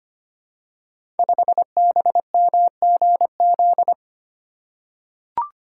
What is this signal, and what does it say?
Morse code, heard as a keyed tone, about 25 wpm, 700 Hz. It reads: 5BMGZ
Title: Code Groups - 5 Letter:Numbers - 25wpm